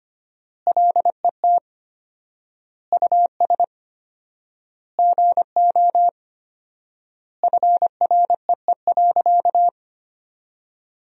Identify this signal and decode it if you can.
Morse code, heard as a keyed tone, about 25 wpm, 700 Hz. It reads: LET US GO FREE.
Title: 4x Word-Spacing - Sentences from Top 500 Words - 25wpm